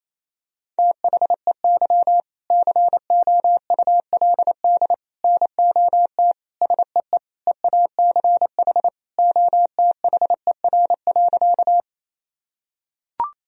Morse code, heard as a keyed tone, about 28 wpm, 700 Hz. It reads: THEY COULD NOT SEE EACH OTHER.
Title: Sentences from Top 1000 Words - Review Part 1 - 28wpm